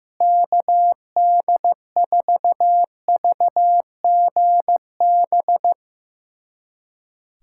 Morse code, heard as a keyed tone, about 15 wpm, 700 Hz. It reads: KD4VGB
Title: USA Callsign DE Callsign 15wpm